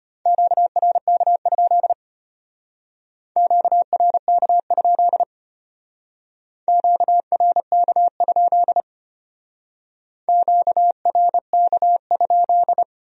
Morse code, heard as a keyed tone, about 32 wpm, 700 Hz. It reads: QRK? QRK? QRK? QRK?